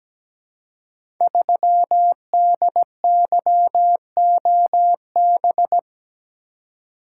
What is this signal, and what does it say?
Morse code, heard as a keyed tone, about 17 wpm, 700 Hz. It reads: 3DYOB
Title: Code Groups - 5 Letter:Numbers - 17wpm